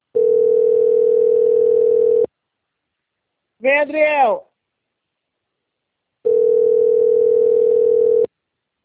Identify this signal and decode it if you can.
Ringback tone (North America), 440 + 480 Hz, with speech.
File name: out- - - - - .91797